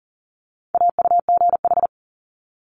Morse code, heard as a keyed tone, about 40 wpm, 690 Hz. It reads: AUGH
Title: Multi-letter Phonemes 40wpm